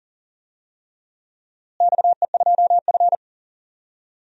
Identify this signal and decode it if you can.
Morse code, heard as a keyed tone, about 37 wpm, 700 Hz. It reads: XE2F
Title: DXCC-100 Call Signs - 4 Characters - Repeated 3x - 40wpm